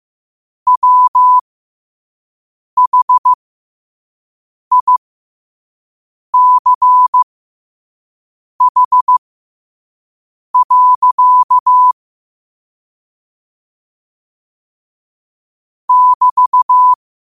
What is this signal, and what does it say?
Morse code, heard as a keyed tone, about 15 wpm, spaced as for 5 wpm, 1000 Hz. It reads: WHICH. <BT>
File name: qotdfiles / 5wpm